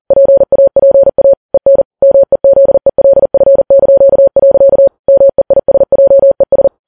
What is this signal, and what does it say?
Morse code, heard as a keyed tone, about 40 wpm, 550 Hz. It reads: PAPA EN MEZELF). MEISJES
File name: Files Web tls / Morse on9fv / morse40kps